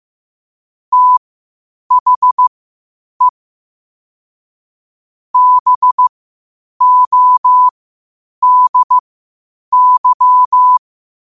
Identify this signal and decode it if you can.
Morse code, heard as a keyed tone, about 15 wpm, spaced as for 8 wpm, 1000 Hz. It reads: THE BODY